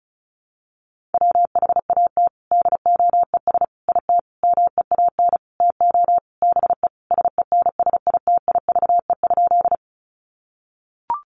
Morse code, heard as a keyed tone, about 34 wpm, 700 Hz. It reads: WHAT DOES IT MEAN TO BE SENSITIVE?
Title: Sentences from Top 1400 Words 35wpm